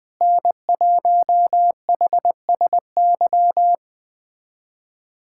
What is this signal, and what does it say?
Morse code, heard as a keyed tone, about 20 wpm, 700 Hz. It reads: N1HSY
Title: USA Callsign DE Callsign - 5 Characters - 20wpm